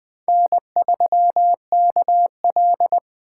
Morse code, heard as a keyed tone, about 21 wpm, 700 Hz. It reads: N3KL
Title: USA Call Signs - 4 Characters - 8x Character Spacing - 20wpm